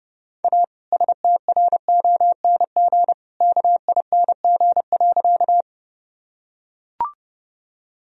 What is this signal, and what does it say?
Morse code, heard as a keyed tone, about 30 wpm, 700 Hz. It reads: A STRONG KING.
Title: Sentences from Top 1000 Words - Review Part 3 - 30wpm